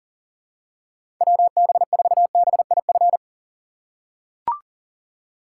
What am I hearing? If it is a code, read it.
Morse code, heard as a keyed tone, about 44 wpm, 700 Hz. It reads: WB4BIF
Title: USA Call Signs - 6 Characters - 6x Character Spacing - 40wpm